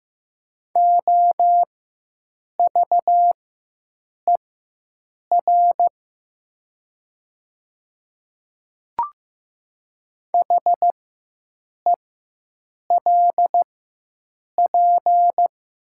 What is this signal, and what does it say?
Morse code, heard as a keyed tone, about 15 wpm, spaced as for 7 wpm, 700 Hz. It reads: OVER HELP
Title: 4x Character Spacing - Top 300 Words - 15wpm